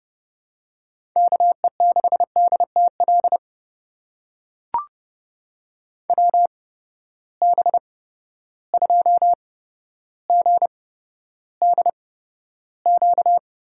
Morse code, heard as a keyed tone, about 32 wpm, spaced as for 13 wpm, 700 Hz. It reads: KE6DTL WB2GDQ
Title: USA Call Signs - 6 Characters - 8x Character Spacing - 30wpm